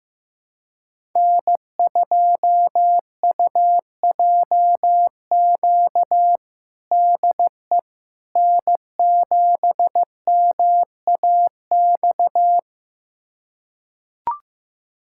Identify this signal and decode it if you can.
Morse code, heard as a keyed tone, about 15 wpm, 700 Hz. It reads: N2UJQ DE N7MAX
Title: USA Callsign DE Callsign - 5 Characters - 15wpm